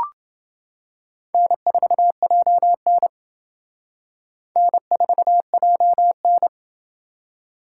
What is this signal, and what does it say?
Morse code, heard as a keyed tone, about 28 wpm, 700 Hz. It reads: N4JN N4JN